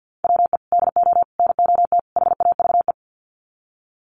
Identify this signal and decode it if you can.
Morse code, heard as a keyed tone, about 50 wpm, 700 Hz. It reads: WE DO NOT HAVE
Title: Sets of 4 Words 50wpm